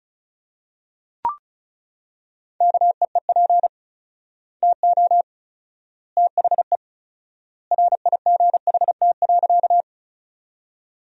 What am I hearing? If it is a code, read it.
Morse code, heard as a keyed tone, about 34 wpm, 700 Hz. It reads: KEEP TO THE RIGHT.
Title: 4x Word-Spacing - Sentences from Top 300 Words - 35wpm